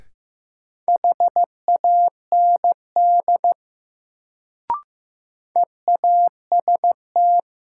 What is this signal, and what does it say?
Morse code, heard as a keyed tone, about 15 wpm, 700 Hz. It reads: HAND EAST
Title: Sets of 3 Words - New Word Drill - 15wpm